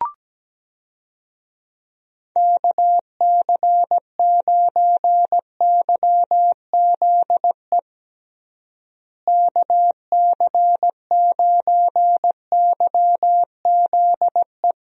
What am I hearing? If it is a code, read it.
Morse code, heard as a keyed tone, about 17 wpm, 700 Hz. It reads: KC9YZE KC9YZE
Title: USA Call Signs - 6 Characters - Repeated 3x - 17wpm